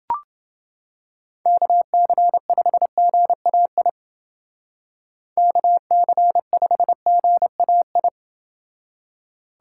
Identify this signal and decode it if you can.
Morse code, heard as a keyed tone, about 28 wpm, 700 Hz. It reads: KC5GAI KC5GAI